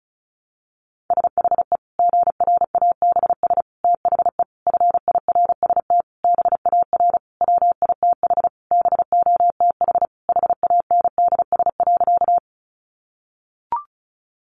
Morse code, heard as a keyed tone, about 34 wpm, 700 Hz. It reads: SHE GRABS THE FIRST BAR WITH BOTH HANDS.
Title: Easy Phrases 35wpm